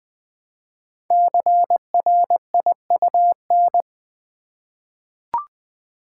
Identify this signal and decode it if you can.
Morse code, heard as a keyed tone, about 20 wpm, 700 Hz. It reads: CRIUN
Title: Code Groups - 5 Letters - 20wpm